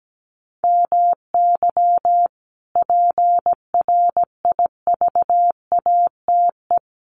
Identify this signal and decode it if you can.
Morse code, heard as a keyed tone, about 17 wpm, 700 Hz. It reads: MY PRIVATE
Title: Sentences from Top 1000 Words 17wpm